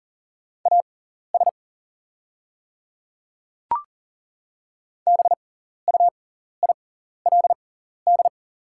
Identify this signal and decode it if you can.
Morse code, heard as a keyed tone, about 40 wpm, spaced as for 14 wpm, 700 Hz. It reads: AS BUILD
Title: 6x Character Spacing - Top 200 Words - 40wpm